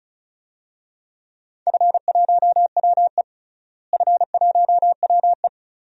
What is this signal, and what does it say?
Morse code, heard as a keyed tone, about 36 wpm, 690 Hz. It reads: F1WE F1WE
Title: DXCC-100 Call Signs - 4 Characters - Repeated 2x - 35wpm